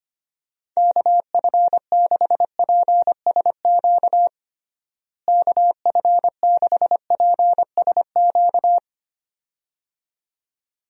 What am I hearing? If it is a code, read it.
Morse code, heard as a keyed tone, about 25 wpm, 700 Hz. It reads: KF6PSQ KF6PSQ
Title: USA Call Signs - 6 Characters - Repeated 2x - 25wpm